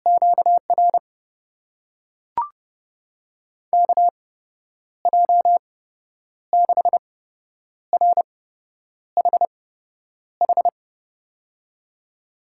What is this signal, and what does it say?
Morse code, heard as a keyed tone, about 30 wpm, spaced as for 9 wpm, 700 Hz. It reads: QR KJ6RHH
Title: USA Call Signs - 6 Characters - 8x Character Spacing - 30wpm